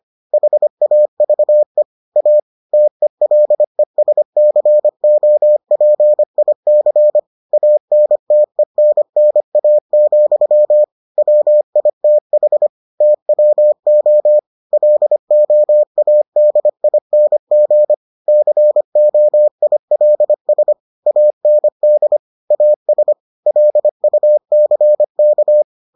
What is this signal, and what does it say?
Morse code, heard as a keyed tone, about 25 wpm, 600 Hz. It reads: HAVE A TELESCOPIC ANTENNA, WITH TWO LOADING COILS AND AS LUCK